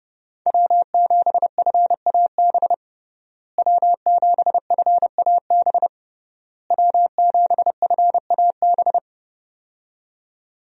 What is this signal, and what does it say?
Morse code, heard as a keyed tone, about 31 wpm, 700 Hz. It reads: W7FAB W7FAB W7FAB
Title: USA Call Signs - 5 Characters - Repeated 3x - 30wpm